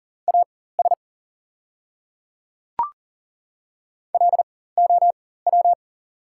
Morse code, heard as a keyed tone, about 40 wpm, spaced as for 19 wpm, 690 Hz. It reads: AS LOW
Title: 4x Character Spacing - Top 200 Words - 40wpm